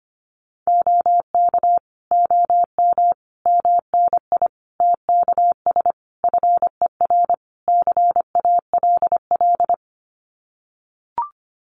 Morse code, heard as a keyed tone, about 25 wpm, 700 Hz. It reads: OK OM MNI TKS FER CALL
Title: Common QSO Phrases 25wpm